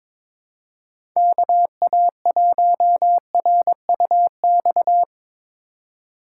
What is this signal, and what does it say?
Morse code, heard as a keyed tone, about 22 wpm, 700 Hz. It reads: KA1RUX
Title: USA Call Signs - 6 Characters - 3x Character Spacing - 22wpm